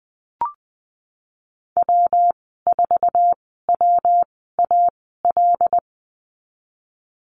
Morse code, heard as a keyed tone, about 19 wpm, spaced as for 14 wpm, 700 Hz. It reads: W4WAL
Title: 140 Lesson 23g - Letter f - Call Signs 20wpm